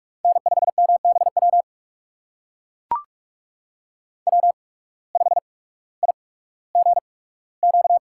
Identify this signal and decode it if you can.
Morse code, heard as a keyed tone, about 44 wpm, spaced as for 19 wpm, 690 Hz. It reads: N5MBW W5IGQ
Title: USA Call Signs - 5 Characters - 8x Character Spacing - 45wpm